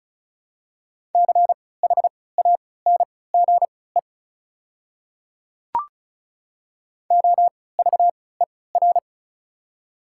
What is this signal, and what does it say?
Morse code, heard as a keyed tone, about 36 wpm, spaced as for 20 wpm, 700 Hz. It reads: CHANGE OVER
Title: 3x Character Spacing - Top 300 Words - 35wpm